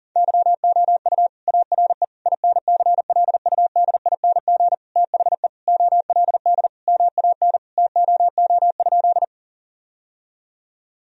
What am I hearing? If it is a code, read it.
Morse code, heard as a keyed tone, about 40 wpm, 700 Hz. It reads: YOU ARE INCLUDING THE OLD MAN TOO?
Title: Sentences from Top 1000 Words - Review Part 4 - 40wpm